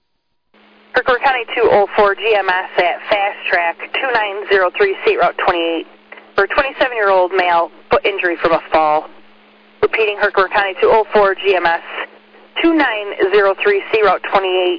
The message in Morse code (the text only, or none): none